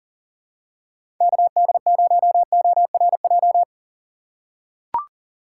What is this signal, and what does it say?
Morse code, heard as a keyed tone, about 40 wpm, 700 Hz. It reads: KD0ORJ
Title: USA Call Signs - 6 Characters - 6x Character Spacing - 40wpm